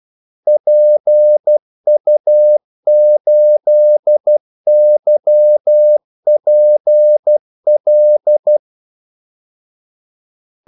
Morse code, heard as a keyed tone, about 12 wpm, 600 Hz. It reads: PU8YPL
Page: 2 ekstra ordmellemrum mellem kaldesignalerne